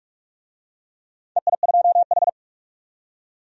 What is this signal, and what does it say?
Morse code, heard as a keyed tone, about 46 wpm, 700 Hz. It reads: EI2H
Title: DXCC-100 Call Signs - 4 Characters - 45wpm